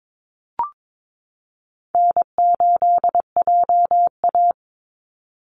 Morse code, heard as a keyed tone, about 22 wpm, 700 Hz. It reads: N8JA
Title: Easy Call Signs - 22wpm